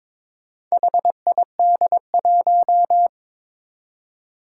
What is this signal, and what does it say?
Morse code, heard as a keyed tone, about 22 wpm, 700 Hz. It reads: HID1
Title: Four Challenging Letter-Number Combinations 22wpm